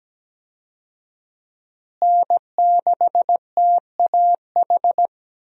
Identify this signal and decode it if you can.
Morse code, heard as a keyed tone, about 17 wpm, 700 Hz. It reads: N6TAH
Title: USA Call Signs - 5 Characters - Repeated 2x - 17wpm